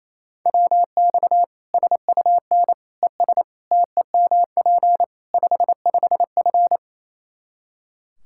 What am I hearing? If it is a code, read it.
Morse code, heard as a keyed tone, about 28 wpm, 700 Hz. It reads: WX SUN ES TEMP 55F
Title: K1USN SST QSO Practice 28wpm